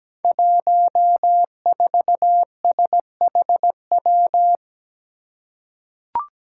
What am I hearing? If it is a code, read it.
Morse code, heard as a keyed tone, about 17 wpm, 700 Hz. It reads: 14SHW
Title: Five Challenging Letter-Number Combinations 17wpm